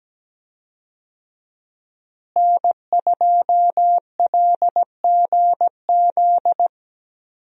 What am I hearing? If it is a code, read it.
Morse code, heard as a keyed tone, about 17 wpm, 700 Hz. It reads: N2LGZ
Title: USA Call Signs - 5 Characters - Repeated 3x - 17wpm